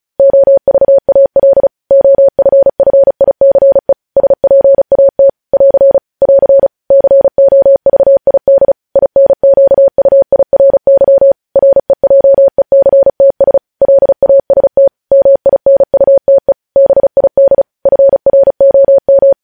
Morse code, heard as a keyed tone, about 36 wpm, 550 Hz. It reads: OVAL OFFICE SPAT <AR> <AR> COVID INQUIRY REJECTS LAST MINUTE BID FROM
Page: News Headlines in Morse Code at 35 WPM